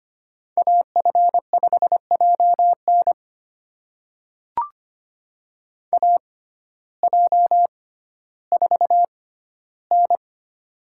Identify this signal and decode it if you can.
Morse code, heard as a keyed tone, about 25 wpm, spaced as for 14 wpm, 700 Hz. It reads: AF5JN AJ4N
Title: USA Call Signs - 6x Character Spacing - 25wpm